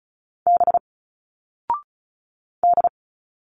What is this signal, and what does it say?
Morse code, heard as a keyed tone, about 35 wpm, 700 Hz. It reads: B D
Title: Single Letters 35wpm